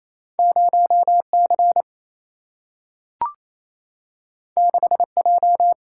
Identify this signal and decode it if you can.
Morse code, heard as a keyed tone, about 28 wpm, 700 Hz. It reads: 0C 6J
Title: Single-Letter Single-Number 28wpm